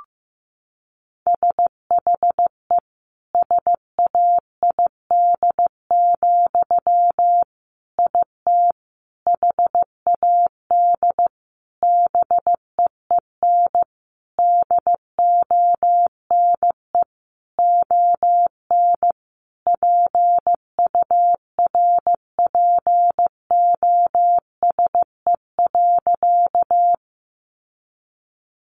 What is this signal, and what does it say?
Morse code, heard as a keyed tone, about 15 wpm, 700 Hz. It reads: SHE SAID, IT HAD BEEN DONE ON PURPOSE.
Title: Sentences from Top 800 Words 15wpm